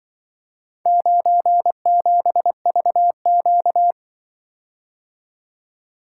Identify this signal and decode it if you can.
Morse code, heard as a keyed tone, about 23 wpm, 700 Hz. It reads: 97VQ
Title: Speed-Racing - Four Letter-Number Combinations - 20wpm